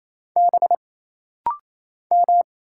Morse code, heard as a keyed tone, about 28 wpm, 700 Hz. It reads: B M
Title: Single Letters - Rapid-Fire 28wpm